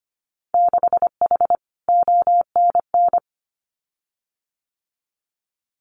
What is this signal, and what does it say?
Morse code, heard as a keyed tone, about 25 wpm, 700 Hz. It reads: 6H ONN
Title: Winter Field Day Exchanges Encore 25wpm